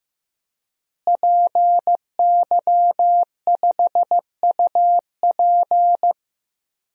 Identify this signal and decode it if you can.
Morse code, heard as a keyed tone, about 15 wpm, 700 Hz. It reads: PY5UP
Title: DXCC-100 Call Signs - 5 Characters - Repeated 3x - 15wpm